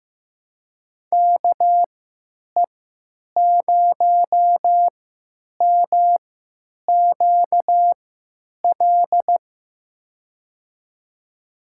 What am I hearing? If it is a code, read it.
Morse code, heard as a keyed tone, about 15 wpm, spaced as for 9 wpm, 700 Hz. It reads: KE0MQL
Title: USA Call Signs - 6 Characters - 3x Character Spacing - 15wpm